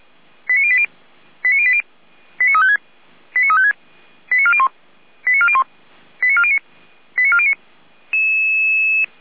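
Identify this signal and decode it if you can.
Five-tone selective call (ZVEI), heard as quick runs of five tones, tones of about 70 ms. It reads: 89080 89080 89356 89356 89401 89401 89409 89409
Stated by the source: ZVEI five-tone selective call received on SDR.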